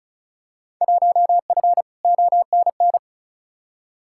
Morse code, heard as a keyed tone, about 36 wpm, 700 Hz. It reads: 1F ONN
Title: Field Day Exchanges Encore - 35wpm